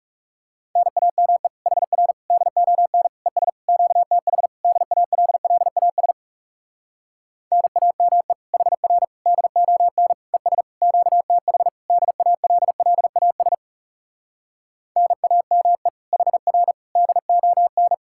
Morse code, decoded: NAME HR DON ES QTH DALLAS NAME HR DON ES QTH DALLAS NAME HR DON

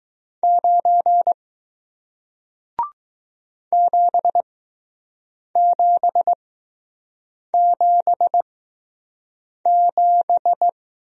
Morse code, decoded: 9 7777

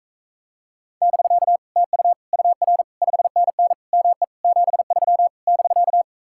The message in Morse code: <BK> TU UR 5NN ME 73 <BK>

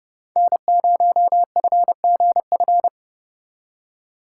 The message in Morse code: N0FGF

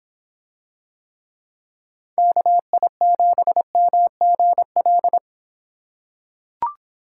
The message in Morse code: KI7MGL

